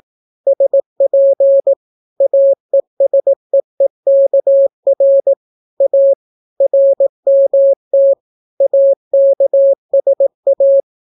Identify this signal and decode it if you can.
Morse code, heard as a keyed tone, about 18 wpm, 550 Hz. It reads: SP AESEEKR A RMT AKSA